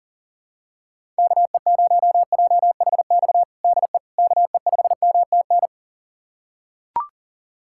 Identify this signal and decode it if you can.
Morse code, heard as a keyed tone, about 40 wpm, 700 Hz. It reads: KE0JHX DE KE5MTN
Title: USA Callsign DE Callsign - 6 Characters - 40wpm